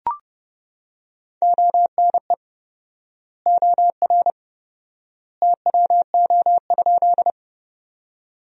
Morse code, heard as a keyed tone, about 30 wpm, 700 Hz. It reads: ONE OR TWO?
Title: 4x Word-Spacing - Sentences from Top 100 Words - 30wpm